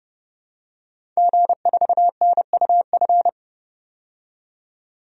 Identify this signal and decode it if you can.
Morse code, heard as a keyed tone, about 30 wpm, 690 Hz. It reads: G4NUF
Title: DXCC-100 Call Signs - 5 Characters - 30wpm